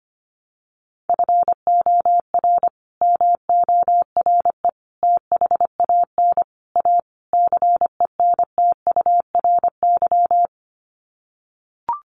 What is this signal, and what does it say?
Morse code, heard as a keyed tone, about 25 wpm, 700 Hz. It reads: FOR MORE THAN A CENTURY
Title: Sets of 5 Words Encore2 25wpm